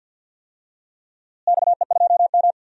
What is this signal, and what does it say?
Morse code, heard as a keyed tone, about 49 wpm, 700 Hz. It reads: XE2M
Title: DXCC-100 Call Signs - 4 Characters - Repeated 3x - 50wpm